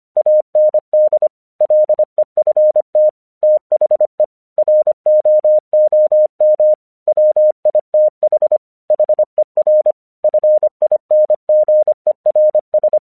Morse code, heard as a keyed tone, about 25 wpm, 610 Hz. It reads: AND LEFT THE ROOM WITH HER FINGERS
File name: 1 English 25wpm / AQC23